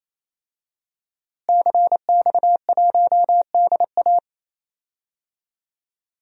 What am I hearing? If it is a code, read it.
Morse code, heard as a keyed tone, about 28 wpm, 700 Hz. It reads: CX1DA
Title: DXCC-100 Call Signs - 5 Characters - Repeated 3x - 28wpm